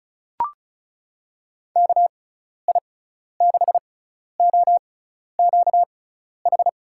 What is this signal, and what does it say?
Morse code, heard as a keyed tone, about 36 wpm, spaced as for 12 wpm, 700 Hz. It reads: KI6OQH
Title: USA Call Signs - 6 Characters - 6x Character Spacing - 35wpm